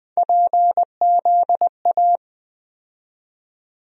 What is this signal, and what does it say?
Morse code, heard as a keyed tone, about 20 wpm, 700 Hz. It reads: PZA
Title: USA Call Sign Postfixes - 3 Letters 20wpm